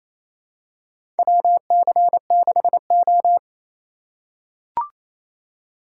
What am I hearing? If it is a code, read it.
Morse code, heard as a keyed tone, about 28 wpm, 700 Hz. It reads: WC6O
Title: USA Call Signs - 4 Characters - 4x Character Spacing - 28wpm